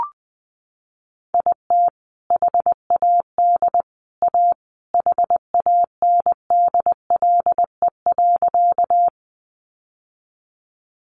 Morse code, heard as a keyed tone, about 19 wpm, 700 Hz. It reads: IT HAD A HANDLE.
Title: Sentences from Top 1100 Words 20wpm